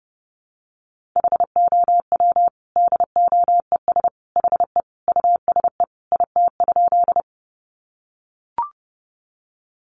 Morse code, heard as a keyed tone, about 30 wpm, 700 Hz. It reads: HOW DOES HE USE IT?
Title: Sentences from Top 200 Words 30wpm